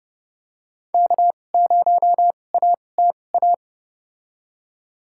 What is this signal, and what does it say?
Morse code, heard as a keyed tone, about 32 wpm, spaced as for 22 wpm, 700 Hz. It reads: K0ATA
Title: USA Call Signs - 5 Characters - 2x Character Spacing - 30wpm